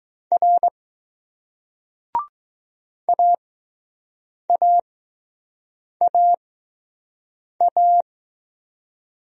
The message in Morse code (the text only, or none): R AAAA